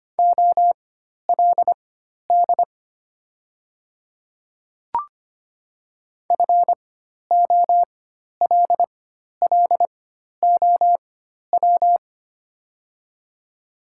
Morse code, heard as a keyed tone, about 25 wpm, spaced as for 12 wpm, 700 Hz. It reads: OLD FOLLOW
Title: 4x Character Spacing - Top 300 Words - 25wpm